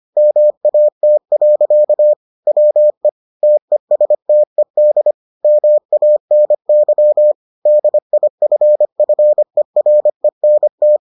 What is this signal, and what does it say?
Morse code, heard as a keyed tone, about 25 wpm, 600 Hz. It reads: MAT. WE TESTED MANY DIFFERENT